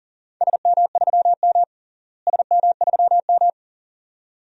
Morse code, heard as a keyed tone, about 40 wpm, 700 Hz. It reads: SM3M SM3M
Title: DXCC-100 Call Signs - 4 Characters - Repeated 2x - 40wpm